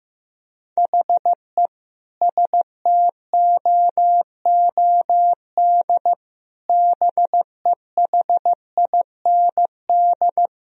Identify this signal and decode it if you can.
Morse code, heard as a keyed tone, about 16 wpm, 700 Hz. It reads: HE STOOD BEHIND
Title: Sentences from Top 1000 Words - Review Part 2 - 15wpm